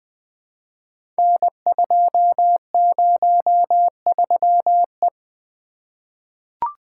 Morse code, heard as a keyed tone, about 20 wpm, 700 Hz. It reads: N203E
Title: Code Groups - 5 Letter:Numbers - 20wpm